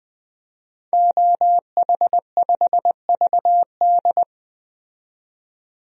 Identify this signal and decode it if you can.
Morse code, heard as a keyed tone, about 20 wpm, 700 Hz. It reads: OH5VD
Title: DXCC-100 Call Signs - 5 Characters - 20wpm